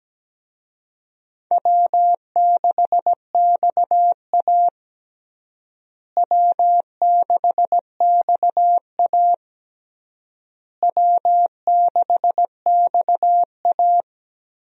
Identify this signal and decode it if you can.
Morse code, heard as a keyed tone, about 17 wpm, 700 Hz. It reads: W6XA W6XA W6XA
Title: USA Call Signs - 4 Characters - Repeated 3x - 17wpm